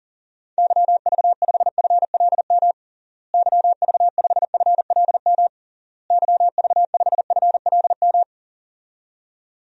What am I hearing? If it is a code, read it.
Morse code, heard as a keyed tone, about 37 wpm, 700 Hz. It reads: YV5FLM YV5FLM YV5FLM
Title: DXCC-100 Call Signs - 6 Characters - Repeated 3x - 40wpm